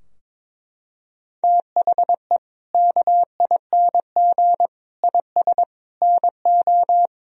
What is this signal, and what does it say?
Morse code, heard as a keyed tone, about 22 wpm, 700 Hz. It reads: THE KING IS NO
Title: Sentences from Top 1000 Words - Review Part 4 - 22wpm